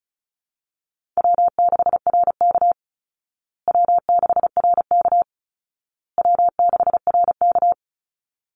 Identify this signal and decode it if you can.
Morse code, heard as a keyed tone, about 34 wpm, 700 Hz. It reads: W6RK W6RK W6RK
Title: Easy Call Signs Repeated 3x - 35wpm